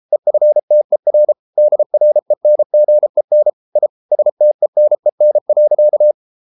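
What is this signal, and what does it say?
Morse code, heard as a keyed tone, about 33 wpm, 600 Hz. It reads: EFTER DRENGEN I STENEN.
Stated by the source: Morse code at 33 wpm, QRQ.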